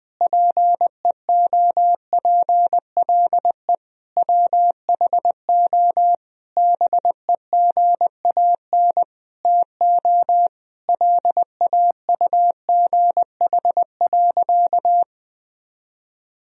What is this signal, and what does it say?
Morse code, heard as a keyed tone, about 20 wpm, 700 Hz. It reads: PEOPLE WHO BEGAN TO LAUGH.
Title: Sentences from Top 1000 Words - Review Part 1 - 20wpm